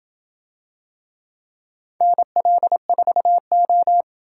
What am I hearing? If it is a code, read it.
Morse code, heard as a keyed tone, about 27 wpm, 700 Hz. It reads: NL4O